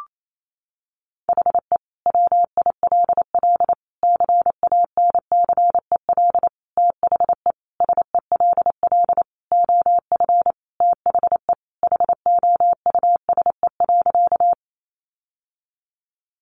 Morse code, heard as a keyed tone, about 28 wpm, 700 Hz. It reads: HE WILL CANCEL THE SELL OF THE HOUSE.
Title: Sentences from Top 1200 Words 28wpm